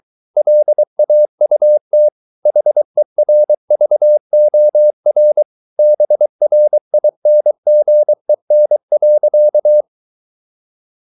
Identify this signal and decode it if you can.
Morse code, heard as a keyed tone, about 23 wpm, 600 Hz. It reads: LAUT HERVORBRINGEN.